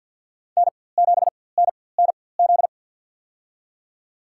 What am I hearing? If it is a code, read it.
Morse code, heard as a keyed tone, about 53 wpm, spaced as for 24 wpm, 700 Hz. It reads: N7NNZ
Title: USA Call Signs - 5 Characters - 4x Character Spacing - 50wpm